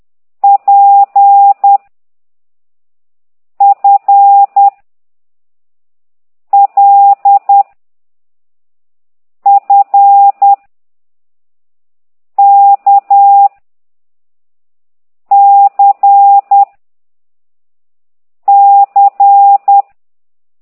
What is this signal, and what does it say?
Morse code, heard as a keyed tone, about 10 wpm, 800 Hz. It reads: P F L F K C C
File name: ct2idw / exerc11